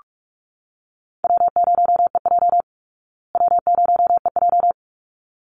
Morse code, heard as a keyed tone, about 44 wpm, 700 Hz. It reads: W0EJ W0EJ
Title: Easy Call Signs Repeated 2x - 45wpm